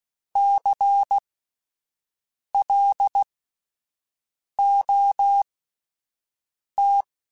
Morse code, heard as a keyed tone, about 16 wpm, spaced as for 5 wpm, 800 Hz. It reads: CLOT
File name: Alphabetical / Alphabetical mp3s / Cs @5